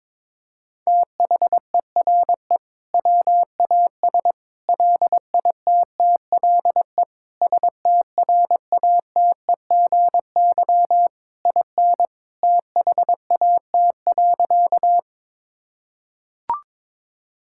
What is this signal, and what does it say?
Morse code, heard as a keyed tone, about 22 wpm, 700 Hz. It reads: THERE WAS LITTLE STRATEGY IN THAT.
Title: Sentences from Top 1000 Words - Review Part 5 - 22wpm